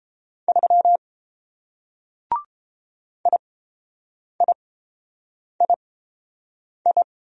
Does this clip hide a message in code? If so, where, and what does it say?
Yes: Morse code from 0.5 to 7.0 s, heard as a keyed tone, about 29 wpm, spaced as for 7 wpm, 700 Hz. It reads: 3 IIII